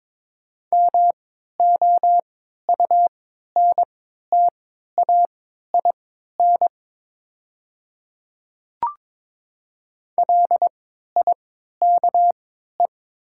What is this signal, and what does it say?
Morse code, heard as a keyed tone, about 22 wpm, spaced as for 13 wpm, 700 Hz. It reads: MOUNTAIN LIKE
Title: 3x Character Spacing - Top 300 Words - 22wpm